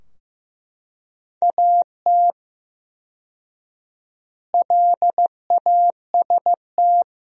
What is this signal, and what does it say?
Morse code, heard as a keyed tone, about 15 wpm, 700 Hz. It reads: AT LAST
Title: 4x Word-Spacing - Sentences from Top 300 Words - 15wpm